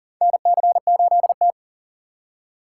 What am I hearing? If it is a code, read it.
Morse code, heard as a keyed tone, about 40 wpm, 700 Hz. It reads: NC8T
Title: USA Call Signs - 4 Characters - 6x Character Spacing - 40wpm